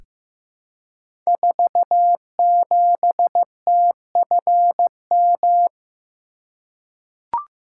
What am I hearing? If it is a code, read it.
Morse code, heard as a keyed tone, about 15 wpm, 700 Hz. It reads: 47TFM
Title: Code Groups - 5 Letter:Numbers - 15wpm